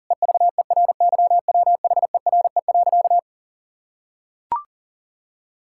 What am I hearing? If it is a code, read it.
Morse code, heard as a keyed tone, about 40 wpm, 700 Hz. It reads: EVERYWHERE.
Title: Sentences from Top 1600 Words 40wpm